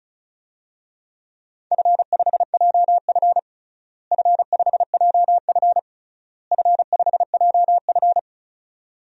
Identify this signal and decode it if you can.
Morse code, heard as a keyed tone, about 36 wpm, 700 Hz. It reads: F5JF F5JF F5JF
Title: DXCC-100 Call Signs - 4 Characters - Repeated 3x - 35wpm